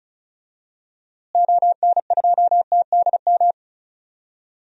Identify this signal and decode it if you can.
Morse code, heard as a keyed tone, about 36 wpm, 700 Hz. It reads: ON2TDM
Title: DXCC-100 Call Signs - 6 Characters - Repeated 2x - 35wpm